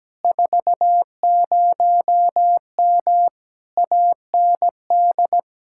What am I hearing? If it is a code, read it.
Morse code, heard as a keyed tone, about 17 wpm, 700 Hz. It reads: 40M AND
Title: Speed-Racing Common QSO Phrases Encore - 17wpm